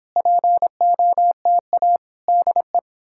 Morse code, heard as a keyed tone, about 30 wpm, 700 Hz. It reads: POTA DE